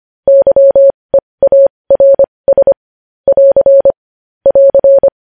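Morse code, heard as a keyed tone, about 25 wpm, spaced as for 20 wpm, 550 Hz. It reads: YEARS <AR> <AR>
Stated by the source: Morse code news headlines.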